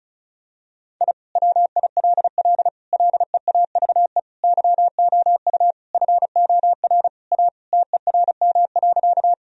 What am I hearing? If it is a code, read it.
Morse code, heard as a keyed tone, about 34 wpm, 700 Hz. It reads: I WILL LEAVE YOU FOR A TERM.
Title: Sentences from Top 1000 Words - Review Part 4 - 35wpm